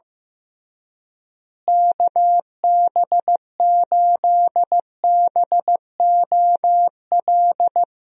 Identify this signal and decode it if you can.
Morse code, heard as a keyed tone, about 15 wpm, 700 Hz. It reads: KB8BOL
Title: USA Call Signs - 6 Characters - Repeated 2x - 15wpm